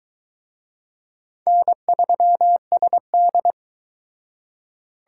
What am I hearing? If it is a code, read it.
Morse code, heard as a keyed tone, about 23 wpm, 700 Hz. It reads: N3SD